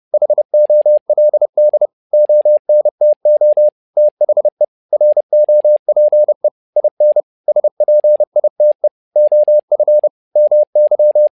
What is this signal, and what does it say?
Morse code, heard as a keyed tone, about 30 wpm, 600 Hz. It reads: HOLD ONTO THE ROPE IN SPITE OF MY
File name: learning center / tower trouble / tower trouble 30wpm